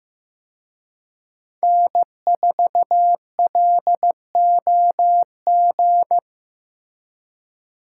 Morse code, heard as a keyed tone, about 18 wpm, 700 Hz. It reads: N4LOG
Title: Speed Racing - USA 1x3 Call Signs 15wpm